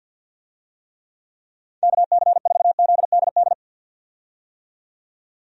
Morse code, heard as a keyed tone, about 52 wpm, 700 Hz. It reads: KC4ZDD
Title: USA Call Signs - 6 Characters - 50wpm